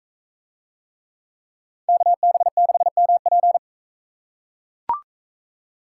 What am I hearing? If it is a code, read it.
Morse code, heard as a keyed tone, about 42 wpm, 700 Hz. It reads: KB6MP